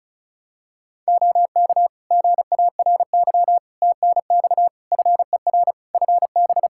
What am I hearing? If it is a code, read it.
Morse code, heard as a keyed tone, about 34 wpm, 700 Hz. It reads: OK GARY TNX FER FB
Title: Practice QSO - Exchange 4 - Ending - 35wpm